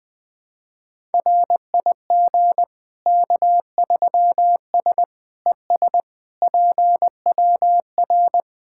Morse code, heard as a keyed tone, about 20 wpm, 700 Hz. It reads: RIG K3S ES PWR
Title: K1USN SST QSO Practice 20wpm